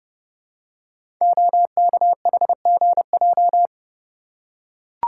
Morse code, heard as a keyed tone, about 30 wpm, 700 Hz. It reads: OKHGJ
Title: Code Groups - 5 Letter:Numbers - 30wpm